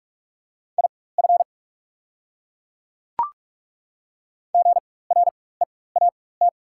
Morse code, heard as a keyed tone, about 43 wpm, spaced as for 21 wpm, 690 Hz. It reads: IF GREAT
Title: 4x Character Spacing - Top 300 Words - 45wpm